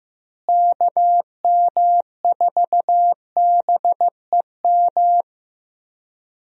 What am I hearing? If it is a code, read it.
Morse code, heard as a keyed tone, about 15 wpm, 700 Hz. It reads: KM4BEM
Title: USA Call Signs - 6 Characters - Repeated 2x - 15wpm